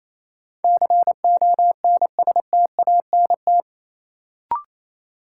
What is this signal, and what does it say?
Morse code, heard as a keyed tone, about 28 wpm, 700 Hz. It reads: CONSTANT
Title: Sentences from Top 1500 Words - New Word Drill - 28wpm